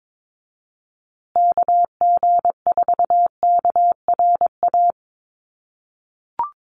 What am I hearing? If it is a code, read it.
Morse code, heard as a keyed tone, about 22 wpm, 700 Hz. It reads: KG4KRA
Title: USA Call Signs Repeated 2x 22wpm